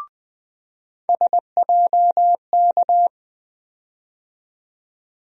Morse code, heard as a keyed tone, about 20 wpm, 700 Hz. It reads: SJK
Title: USA Call Sign Postfixes - 3 Letters 20wpm